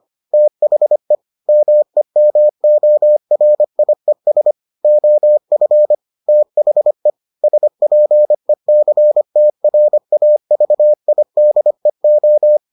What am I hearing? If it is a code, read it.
Morse code, heard as a keyed tone, about 25 wpm, 600 Hz. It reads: THE MEMORIES OF THE SPECTRAVIDEO